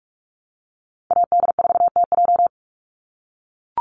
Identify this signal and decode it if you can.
Morse code, heard as a keyed tone, about 44 wpm, 700 Hz. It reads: AD4TJ
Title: Call Signs - 45wpm